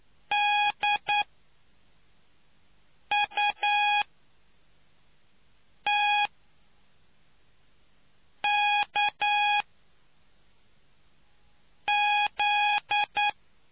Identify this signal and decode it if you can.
Morse code, heard as a keyed tone, about 9 wpm, spaced as for 4 wpm, 830 Hz. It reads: DUTKZ